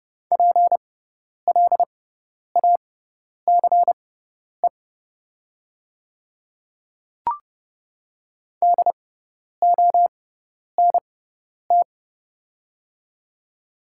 Morse code, heard as a keyed tone, about 30 wpm, spaced as for 10 wpm, 690 Hz. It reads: PLACE DONT
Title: 6x Character Spacing - Top 300 Words - 30wpm